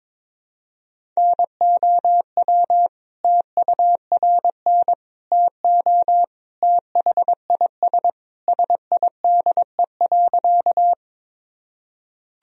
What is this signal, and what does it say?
Morse code, heard as a keyed tone, about 22 wpm, 700 Hz. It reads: NOW TURN TO THIS SIDE.
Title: Sentences from Top 1000 Words - Review Part 1 - 22wpm